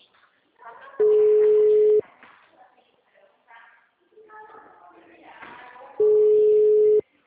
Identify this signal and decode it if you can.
Ringback tone (Europe, 425 Hz), with speech.
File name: out- -2002- - - .1549